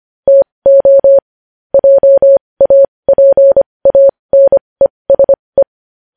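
Morse code, heard as a keyed tone, about 25 wpm, spaced as for 20 wpm, 550 Hz. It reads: TO JAPANESE
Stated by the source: Morse code practice sending news headlines at 20 wpm.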